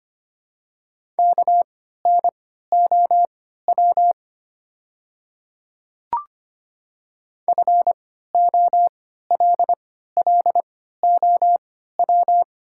Morse code, heard as a keyed tone, about 25 wpm, spaced as for 14 wpm, 700 Hz. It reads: KNOW FOLLOW
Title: 3x Character Spacing - Top 300 Words - 25wpm